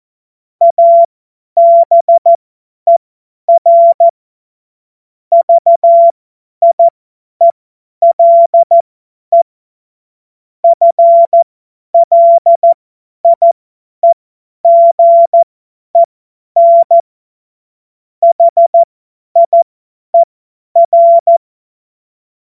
Morse code, heard as a keyed tone, about 14 wpm, spaced as for 10 wpm, 680 Hz. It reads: ABER VIELE FLIEGEN HIER
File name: Witz-14-Oktober-2025-50-F70